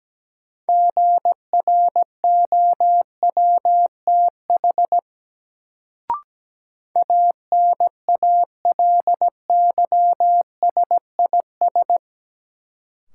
Morse code, heard as a keyed tone, about 17 wpm, 700 Hz. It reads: GROWTH ANALYSIS
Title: Sentences from Top 1000 Words - Review Part 5 - New Word Drill - 17wpm